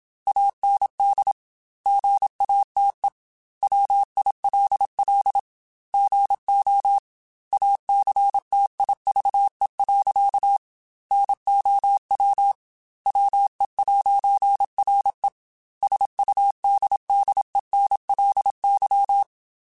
Morse code, heard as a keyed tone, about 27 wpm, 800 Hz. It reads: AND GATE WILL GO ACTIVE. NOW WE'RE SUDDENLY
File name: Technical Subjects / Technical Subjects-mp3s / AND Gate @25